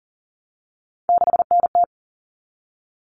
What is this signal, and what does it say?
Morse code, heard as a keyed tone, about 40 wpm, 700 Hz. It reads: 6NT
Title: All Cut Numbers - 3-Digit 40wpm